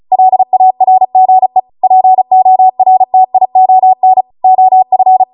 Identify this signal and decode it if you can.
Morse code, heard as a keyed tone, about 36 wpm, 750 Hz. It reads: LARGE PORTION OF
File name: Morse / 35WPM